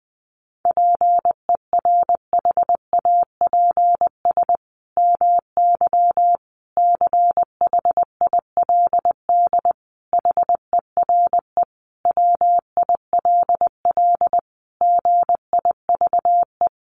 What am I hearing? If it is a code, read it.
Morse code, heard as a keyed tone, about 20 wpm, 700 Hz. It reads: PERHAPS MY CHILD HERE WILL GIVE
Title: Sentences from Top 1100 Words 20wpm